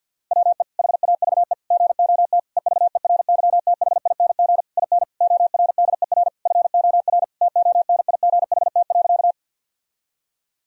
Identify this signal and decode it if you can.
Morse code, heard as a keyed tone, about 47 wpm, 700 Hz. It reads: WE HAVE GOT EVERYTHING IN ORDER FOR TONIGHT.
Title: Sentences from Top 1000 Words - Review Part 3 - 50wpm